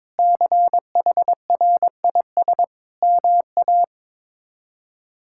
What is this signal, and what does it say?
Morse code, heard as a keyed tone, about 21 wpm, 700 Hz. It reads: CHRIS MA
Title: K1USN SST QSO Practice - 22wpm